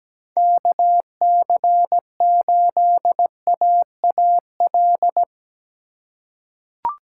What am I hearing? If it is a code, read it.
Morse code, heard as a keyed tone, about 17 wpm, 700 Hz. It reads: KC8AAL
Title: USA Call Signs - 6 Characters - 3x Character Spacing - 17wpm